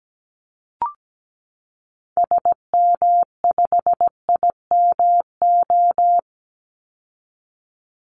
Morse code, heard as a keyed tone, about 17 wpm, 700 Hz. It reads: SM5IMO
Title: Call Signs Repeated 3x - 17wpm